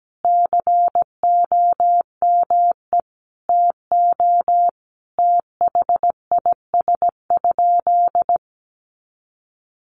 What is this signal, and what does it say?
Morse code, heard as a keyed tone, about 17 wpm, 700 Hz. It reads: COME TO THIS?
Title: Sentences from Top 100 Words 17wpm